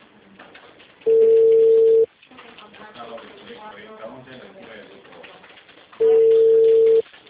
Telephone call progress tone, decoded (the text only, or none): ringback tone